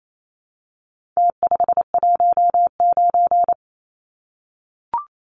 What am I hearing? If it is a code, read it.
Morse code, heard as a keyed tone, about 28 wpm, 700 Hz. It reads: T519
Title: Numbers - Contest Style - 4-Digit 28wpm